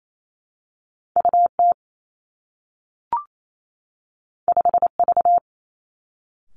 Morse code, heard as a keyed tone, about 28 wpm, 700 Hz. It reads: U T 5 V
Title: Two Challenging Letter-Number Combinations 28wpm